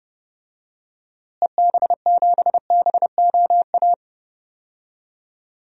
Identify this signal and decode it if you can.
Morse code, heard as a keyed tone, about 31 wpm, 700 Hz. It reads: EB7BOA
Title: DXCC-100 Call Signs - 6 Characters - Repeated 3x - 30wpm